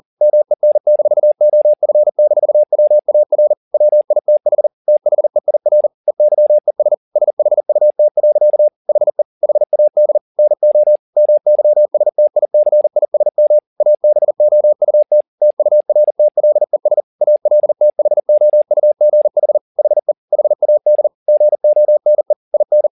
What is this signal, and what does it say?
Morse code, heard as a keyed tone, about 40 wpm, 600 Hz. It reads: MEN-OF-WAR WITH THEIR EYES SHUT. HE HAD NO MYSTICISM ABOUT TURTLES ALTHOUGH HE HAD GONE IN